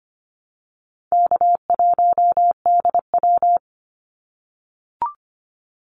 Morse code, heard as a keyed tone, about 25 wpm, 700 Hz. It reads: K1DW
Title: Easy Call Signs - 25wpm